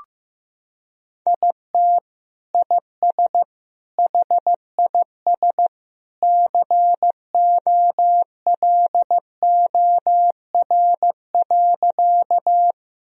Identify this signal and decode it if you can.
Morse code, heard as a keyed tone, about 16 wpm, 700 Hz. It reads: IT IS HIS COLOR.
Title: Sentences from Top 1000 Words - Review Part 2 - 15wpm